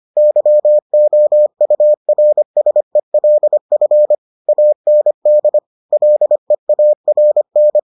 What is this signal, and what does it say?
Morse code, heard as a keyed tone, about 25 wpm, 600 Hz. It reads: YOURSELF AND LEARN